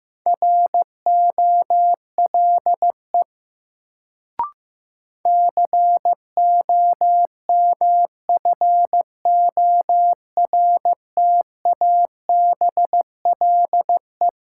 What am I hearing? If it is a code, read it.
Morse code, heard as a keyed tone, about 15 wpm, 700 Hz. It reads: ROLE COMFORTABLE
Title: Sentences from Top 1500 Words - New Word Drill Alt - 15wpm